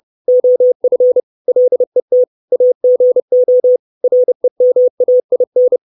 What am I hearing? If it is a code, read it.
Morse code, heard as a keyed tone, about 30 wpm, 500 Hz. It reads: OF LET AGO REMAIN
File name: 632 Words 30WPM